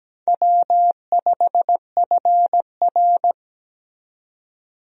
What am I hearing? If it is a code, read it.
Morse code, heard as a keyed tone, about 17 wpm, 700 Hz. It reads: W5FR
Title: USA Call Signs - 4 Characters - 17wpm